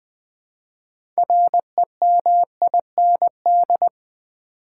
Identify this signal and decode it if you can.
Morse code, heard as a keyed tone, about 20 wpm, 700 Hz. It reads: REMIND
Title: Build A Word - 3 Letters At A Time - 6-Letter Words - 20wpm